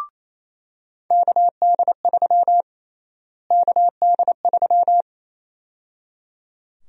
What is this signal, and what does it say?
Morse code, heard as a keyed tone, about 28 wpm, 700 Hz. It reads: KD3 KD3
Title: USA Call Sign Prefixes - Repeated 2x - 28wpm